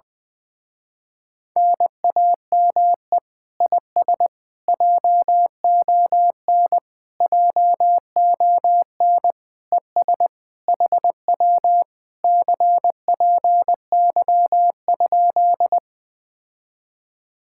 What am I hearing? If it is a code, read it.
Morse code, heard as a keyed tone, about 20 wpm, 700 Hz. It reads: NAME IS JON JON ES HW CPY?
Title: Speed-Racing Common QSO Phrases - 17wpm